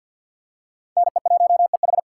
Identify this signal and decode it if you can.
Morse code, heard as a keyed tone, about 51 wpm, 700 Hz. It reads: NE1EH
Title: USA Call Signs - 3x Character Spacing - 50wpm